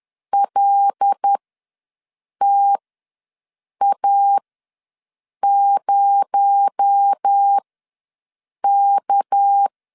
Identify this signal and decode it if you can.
Morse code, heard as a keyed tone, about 10 wpm, spaced as for 6 wpm, 800 Hz. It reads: LTA0K
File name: LEZIONI / 7 GF9Z / 7 G ws 10wpm 3843